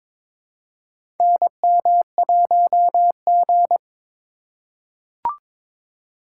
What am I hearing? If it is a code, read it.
Morse code, heard as a keyed tone, about 23 wpm, 700 Hz. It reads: NM1G
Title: USA Call Signs - 4 Characters - 3x Character Spacing - 22wpm